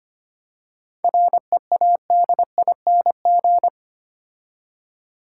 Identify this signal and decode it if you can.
Morse code, heard as a keyed tone, about 30 wpm, 700 Hz. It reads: READING
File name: Easy Words - Speed-Racing with Hints - 25wpm